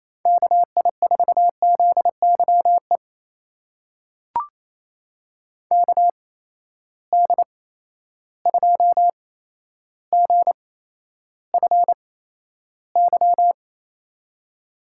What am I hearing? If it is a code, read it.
Morse code, heard as a keyed tone, about 27 wpm, spaced as for 12 wpm, 700 Hz. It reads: KI4ZYE KD2GFY
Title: USA Call Signs - 8x Character Spacing - 28wpm